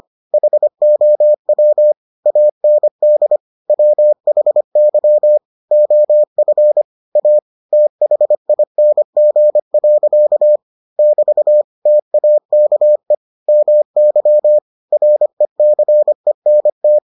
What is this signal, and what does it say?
Morse code, heard as a keyed tone, about 25 wpm, 600 Hz. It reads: HOW AND WHY OF A THING. <BT> TAKE MY RECENT